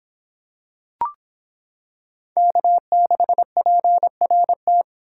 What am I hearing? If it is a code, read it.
Morse code, heard as a keyed tone, about 26 wpm, 700 Hz. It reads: K6PRT